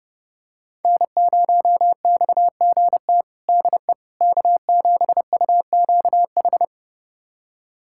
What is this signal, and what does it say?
Morse code, heard as a keyed tone, about 30 wpm, 700 Hz. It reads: N0XGT DE K7UQH
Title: USA Callsign DE Callsign - 5 Characters - 30wpm